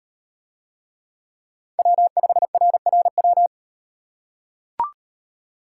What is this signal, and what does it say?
Morse code, heard as a keyed tone, about 38 wpm, 700 Hz. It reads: W5RRW